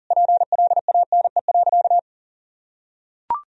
Morse code, heard as a keyed tone, about 37 wpm, 700 Hz. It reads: PLANE.
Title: Sentences from Top 1000 Words - Review Part 3 - 40wpm